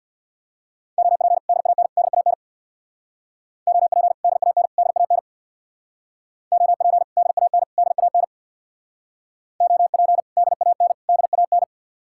Morse code, heard as a keyed tone, about 62 wpm, 700 Hz. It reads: OP DAN DAN OP DAN DAN OP DAN DAN OP DAN DAN
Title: Speed-Racing Common QSO Phrases - 50wpm